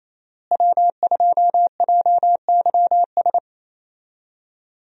Morse code, heard as a keyed tone, about 28 wpm, 700 Hz. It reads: W2JYS
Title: USA Call Signs - 5 Characters - Repeated 2x - 28wpm